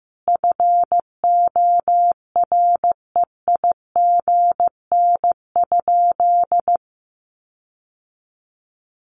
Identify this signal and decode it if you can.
Morse code, heard as a keyed tone, about 15 wpm, 700 Hz. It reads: FOREIGN?
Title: Sentences from Top 1400 Words 15wpm